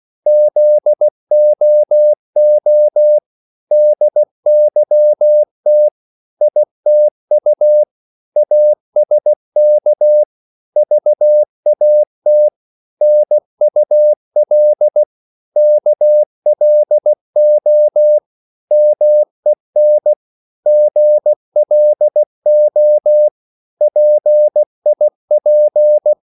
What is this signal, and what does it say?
Morse code, heard as a keyed tone, about 16 wpm, 600 Hz. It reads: ZOO DYT ITU ASK VAT NUL KLO MEN GLO PIP
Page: Korte danske ord 16wpm